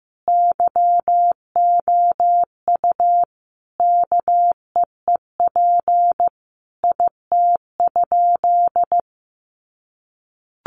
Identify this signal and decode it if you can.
Morse code, heard as a keyed tone, about 15 wpm, 700 Hz. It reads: YOU KEEP IT?
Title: Sentences from Top 300 Words 15wpm